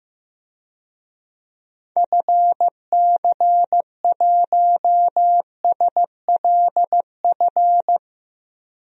Morse code, heard as a keyed tone, about 15 wpm, 700 Hz. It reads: FC1SLF
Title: DXCC-100 Call Signs - 6 Characters - 15wpm